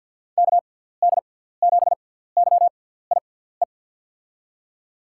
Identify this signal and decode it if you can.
Morse code, heard as a keyed tone, about 48 wpm, spaced as for 17 wpm, 700 Hz. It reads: KD7YIE
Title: USA Call Signs - 6 Characters - 6x Character Spacing - 50wpm